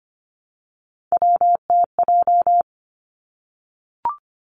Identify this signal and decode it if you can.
Morse code, heard as a keyed tone, about 25 wpm, 700 Hz. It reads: WTJ
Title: Three Challenging Letter-Number Combinations 25wpm